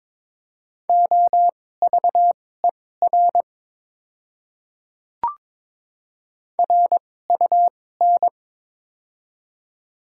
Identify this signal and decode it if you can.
Morse code, heard as a keyed tone, about 22 wpm, spaced as for 16 wpm, 700 Hz. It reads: OVER RUN
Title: 2x Character Spacing - Top 300 Words - 22wpm